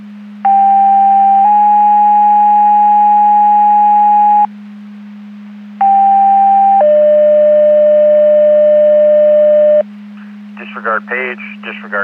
Two-tone page, heard as a one-second tone, then a three-second tone, about 790 then 830 Hz, about 790 then 600 Hz. It is heard from 0.4 to 9.8 s.
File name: Portage 154.385 Paging